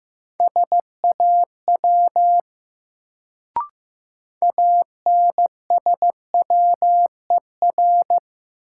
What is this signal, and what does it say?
Morse code, heard as a keyed tone, about 15 wpm, 700 Hz. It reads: SAW ANSWER
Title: Sentences from Top 300 Words - New Word Drill - 15wpm